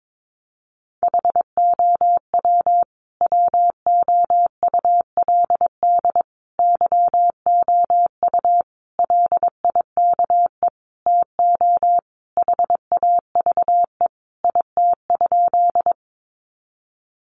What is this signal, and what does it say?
Morse code, heard as a keyed tone, about 22 wpm, 700 Hz. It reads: HOW WOULD YOU LIKE TO HAVE IT?
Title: Sentences from Top 100 Words 22wpm